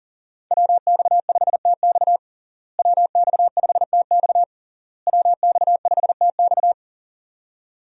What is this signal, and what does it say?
Morse code, heard as a keyed tone, about 37 wpm, 700 Hz. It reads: WX5TX WX5TX WX5TX
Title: USA Call Signs - 5 Characters - Repeated 3x - 40wpm